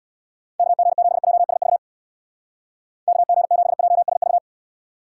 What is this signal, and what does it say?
Morse code, heard as a keyed tone, about 67 wpm, 700 Hz. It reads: KG7JSV KG7JSV